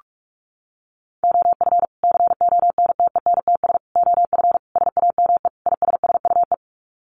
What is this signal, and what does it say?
Morse code, heard as a keyed tone, about 44 wpm, 700 Hz. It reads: OF CONTENTS OF SAME ISSUE
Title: Sets of 5 Words Encore2 45wpm